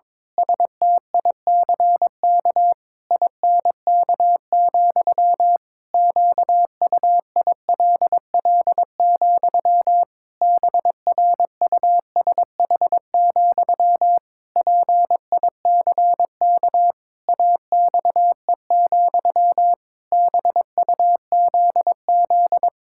Morse code, heard as a keyed tone, about 22 wpm, 700 Hz. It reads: STICK INK, QUILL, BRUSH, PICK AXE, BUZZ
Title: Pangram Challenge v2 - 22wpm